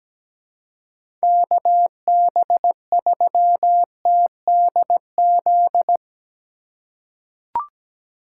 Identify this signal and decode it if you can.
Morse code, heard as a keyed tone, about 17 wpm, 700 Hz. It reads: KB3TDZ
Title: USA Call Signs - 6 Characters - 8x Character Spacing - 17wpm